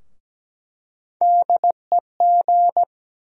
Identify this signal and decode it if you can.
Morse code, heard as a keyed tone, about 17 wpm, 700 Hz. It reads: DEG
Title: USA Call Sign Postfixes - 3 Letters 17wpm